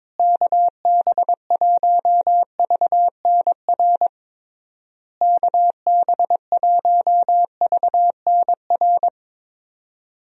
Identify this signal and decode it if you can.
Morse code, heard as a keyed tone, about 22 wpm, 700 Hz. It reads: KB1VNR KB1VNR
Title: USA Call Signs - 6 Characters - Repeated 3x - 22wpm